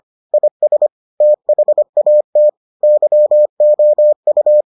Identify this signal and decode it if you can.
Morse code, heard as a keyed tone, about 25 wpm, 600 Hz. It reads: IS THAT YOU